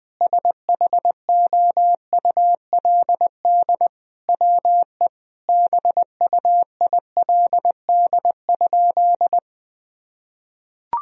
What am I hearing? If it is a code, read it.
Morse code, heard as a keyed tone, about 20 wpm, 700 Hz. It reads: SHOULD WE BUILD?
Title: Sentences from Top 1000 Words - Review Part 3 - 20wpm